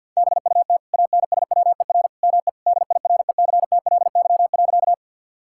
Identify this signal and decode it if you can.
Morse code, heard as a keyed tone, about 46 wpm, 700 Hz. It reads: BUT ANSWER ME DIRECTLY.
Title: Sentences from Top 1000 Words - Review Part 5 - 50wpm